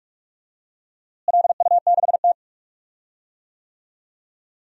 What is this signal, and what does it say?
Morse code, heard as a keyed tone, about 46 wpm, 700 Hz. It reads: LU6T
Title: DXCC-100 Call Signs - 4 Characters - Repeated 2x - 45wpm